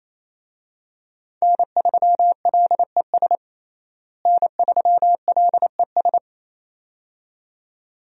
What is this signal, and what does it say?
Morse code, heard as a keyed tone, about 28 wpm, 700 Hz. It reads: N3LES N3LES
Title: USA Call Signs - 5 Characters - Repeated 2x - 28wpm